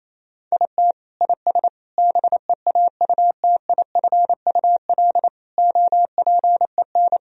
Morse code, heard as a keyed tone, about 27 wpm, 700 Hz. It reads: IT IS BEAUTIFUL OPEN
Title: Sentences from Top 1000 Words - Review Part 2 - 28wpm